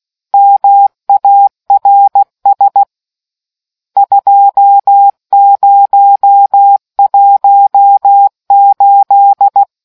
Morse code, heard as a keyed tone, about 16 wpm, 800 Hz. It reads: MARS 2018